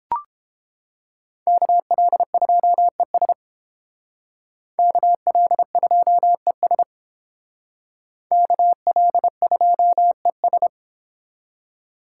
Morse code, heard as a keyed tone, about 30 wpm, 700 Hz. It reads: KL2ES KL2ES KL2ES